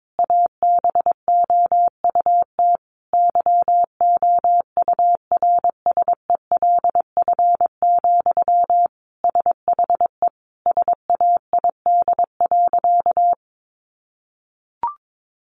Morse code, heard as a keyed tone, about 22 wpm, 700 Hz. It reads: ABOUT YOURSELF, SHE SAID.
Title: Sentences from Top 600 Words 22wpm